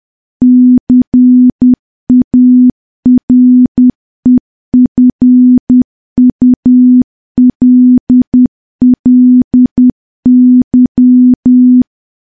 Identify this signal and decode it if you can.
Morse code, heard as a keyed tone, about 10 wpm, 250 Hz. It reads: CAREFULLY
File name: morse / morse-training-10WPM